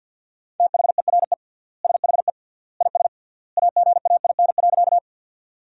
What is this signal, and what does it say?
Morse code, heard as a keyed tone, about 50 wpm, 700 Hz. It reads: THERE SHE IS AGAIN.
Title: 3x Word-Spacing - Sentences from Top 200 Words - 50wpm